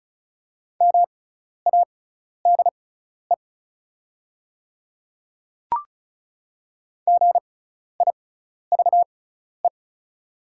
Morse code, heard as a keyed tone, about 36 wpm, spaced as for 12 wpm, 690 Hz. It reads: MADE GIVE
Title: 6x Character Spacing - Top 300 Words - 35wpm